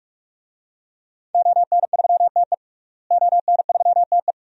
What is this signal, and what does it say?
Morse code, heard as a keyed tone, about 41 wpm, 700 Hz. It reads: ON3TE ON3TE
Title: DXCC-100 Call Signs - 5 Characters - Repeated 2x - 45wpm